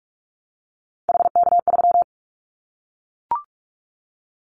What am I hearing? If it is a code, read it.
Morse code, heard as a keyed tone, about 44 wpm, 700 Hz. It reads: HK3
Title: Callsign Prefixes for Top 100 DXCC - 45wpm